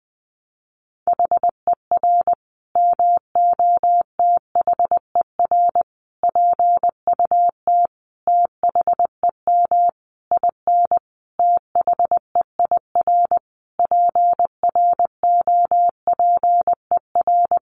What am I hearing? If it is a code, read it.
Morse code, heard as a keyed tone, about 20 wpm, 700 Hz. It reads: HER MOTHER PUT THEM IN THEIR PROPER
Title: Sentences from Top 800 Words 20wpm